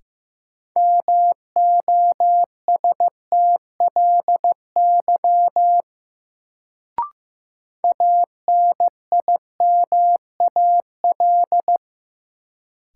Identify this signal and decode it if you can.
Morse code, heard as a keyed tone, about 15 wpm, 700 Hz. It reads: MOSTLY ANIMAL
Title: Sentences from Top 1300 Words - New Word Drill Alt - 15wpm